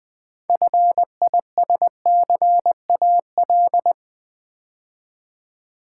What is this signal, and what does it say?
Morse code, heard as a keyed tone, about 20 wpm, 690 Hz. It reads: FISCAL
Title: Build A Word - 2 Letters At A Time - 6-Letter Words - 20wpm